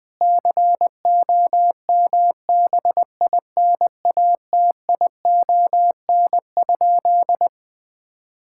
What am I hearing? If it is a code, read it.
Morse code, heard as a keyed tone, about 21 wpm, 700 Hz. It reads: COMBINATION?
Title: Sentences from Top 1000 Words - Review Part 6 - 20wpm